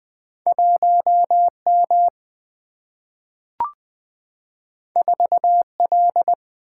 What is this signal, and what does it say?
Morse code, heard as a keyed tone, about 20 wpm, 700 Hz. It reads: 1M 4L
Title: Single-Letter Single-Number 20wpm